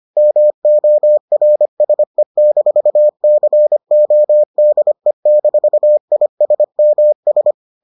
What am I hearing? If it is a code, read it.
Morse code, heard as a keyed tone, about 25 wpm, 600 Hz. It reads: MORSE-CODE-ISMS